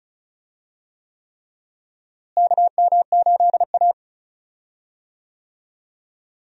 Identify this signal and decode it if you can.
Morse code, heard as a keyed tone, about 36 wpm, 700 Hz. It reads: KM8A
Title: USA Call Signs - 4 Characters - Repeated 3x - 35wpm